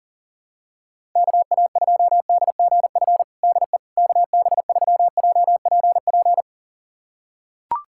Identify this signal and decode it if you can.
Morse code, heard as a keyed tone, about 37 wpm, 700 Hz. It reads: KA2DGF DE KB3JPP
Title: USA Callsign DE Callsign - 6 Characters - 40wpm